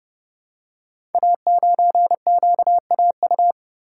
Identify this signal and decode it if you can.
Morse code, heard as a keyed tone, about 30 wpm, 700 Hz. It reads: A9QAU
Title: Code Groups - 5 Letter:Numbers - 30wpm